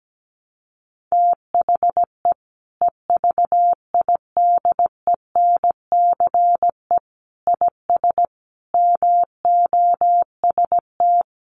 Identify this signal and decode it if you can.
Morse code, heard as a keyed tone, about 17 wpm, 700 Hz. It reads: THE EVIDENCE IS MOST
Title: Sentences from Top 1300 Words 17wpm